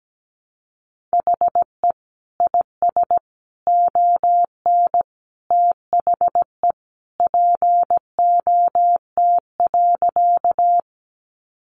Morse code, heard as a keyed tone, about 17 wpm, 700 Hz. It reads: HE IS ON THE POT.
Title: Sentences from Top 1000 Words 17wpm